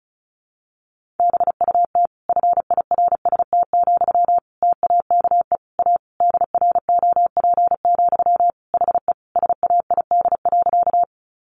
Morse code, heard as a keyed tone, about 34 wpm, 700 Hz. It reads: BUT FIRST, TAKE A DROP, HE SAID.
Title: Sentences from Top 1400 Words 35wpm